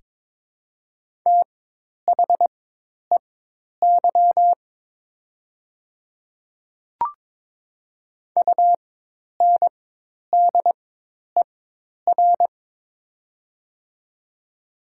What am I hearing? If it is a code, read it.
Morse code, heard as a keyed tone, about 22 wpm, spaced as for 10 wpm, 700 Hz. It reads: THEY UNDER
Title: 4x Character Spacing - Top 300 Words - 22wpm